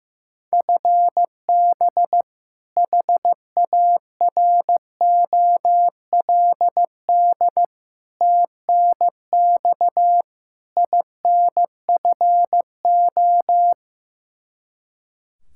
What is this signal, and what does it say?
Morse code, heard as a keyed tone, about 15 wpm, 700 Hz. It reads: FB HAROLD TNX INFO
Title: Practice QSO - Exchange 3 - More Info - 15wpm